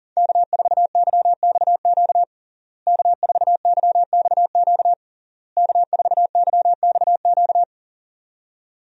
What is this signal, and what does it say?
Morse code, heard as a keyed tone, about 37 wpm, 700 Hz. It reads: K4YXQ K4YXQ K4YXQ
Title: USA Call Signs - 5 Characters - Repeated 3x - 40wpm